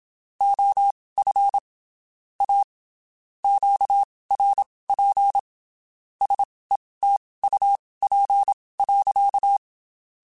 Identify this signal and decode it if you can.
Morse code, heard as a keyed tone, about 27 wpm, spaced as for 19 wpm, 800 Hz. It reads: OF A QRP SETUP.
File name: QSOs / QSOs-mp3s / WD0BGP 1@25